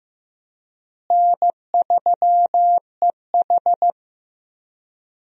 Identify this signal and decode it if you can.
Morse code, heard as a keyed tone, about 16 wpm, 700 Hz. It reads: N3EH
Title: USA Call Signs - 4 Characters - Repeated 3x - 15wpm